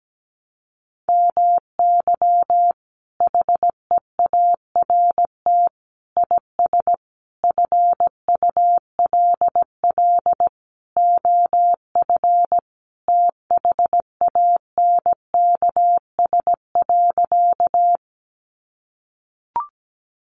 Morse code, heard as a keyed tone, about 17 wpm, 700 Hz. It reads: MY HEART IS FULL OF THANKS.
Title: Sentences from Top 1200 Words 17wpm